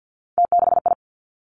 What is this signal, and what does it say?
Morse code, heard as a keyed tone, about 50 wpm, 700 Hz. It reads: T6I
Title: Three Challenging Letter-Number Combinations 50wpm